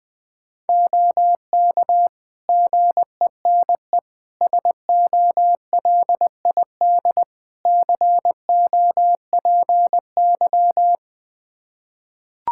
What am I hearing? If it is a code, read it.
Morse code, heard as a keyed tone, about 20 wpm, 700 Hz. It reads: OK GENE SOLID COPY
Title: Practice QSO - Exchange 3 - More Info - 20wpm